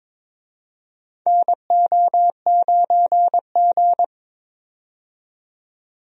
Morse code, heard as a keyed tone, about 22 wpm, 700 Hz. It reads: NO9G
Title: USA Call Signs - 4 Characters - 22wpm